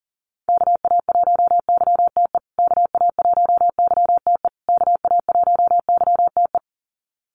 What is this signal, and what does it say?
Morse code, heard as a keyed tone, about 40 wpm, 700 Hz. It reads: KA1YTE KA1YTE KA1YTE
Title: USA Call Signs Repeated 3x 40wpm